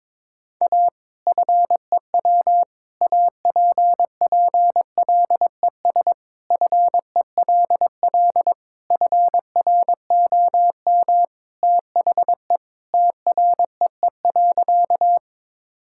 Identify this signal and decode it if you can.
Morse code, heard as a keyed tone, about 22 wpm, 700 Hz. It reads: A FEW APPLES FELL FROM THE TREE.
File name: practice / 22wpm / Sentences with 2-Word Phrases - 22wpm